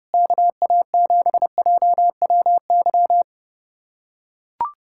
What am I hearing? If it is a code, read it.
Morse code, heard as a keyed tone, about 30 wpm, 700 Hz. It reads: KA7JWY
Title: USA Callsign DE Callsign - 6 Characters - 30wpm